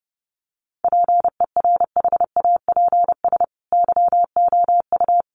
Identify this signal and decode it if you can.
Morse code, heard as a keyed tone, about 30 wpm, 700 Hz. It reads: PERHAPS YOU
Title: Sentences from Top 500 Words 30wpm